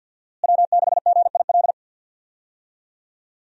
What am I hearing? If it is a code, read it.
Morse code, heard as a keyed tone, about 53 wpm, 700 Hz. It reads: W6GIL
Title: USA Call Signs - 5 Characters - Repeated 2x - 50wpm